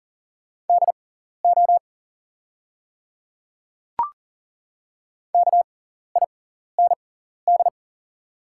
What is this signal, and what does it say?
Morse code, heard as a keyed tone, about 40 wpm, spaced as for 14 wpm, 690 Hz. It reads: DO KIND
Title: 6x Character Spacing - Top 200 Words - 40wpm